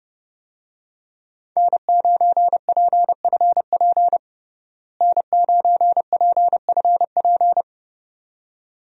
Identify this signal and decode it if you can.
Morse code, heard as a keyed tone, about 30 wpm, 700 Hz. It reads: N9PFP N9PFP
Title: USA Call Signs - 5 Characters - Repeated 2x - 30wpm